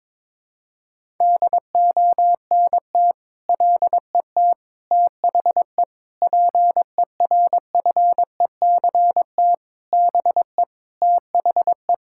Morse code, heard as a keyed tone, about 22 wpm, 700 Hz. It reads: DONT LET THE PERFECT BE THE
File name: Sentences with 9-Word Phrases - 22wpm